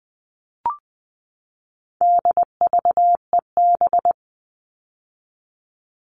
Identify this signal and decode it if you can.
Morse code, heard as a keyed tone, about 20 wpm, 700 Hz. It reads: DVEB
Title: All Cut Numbers - 4-Digit 20wpm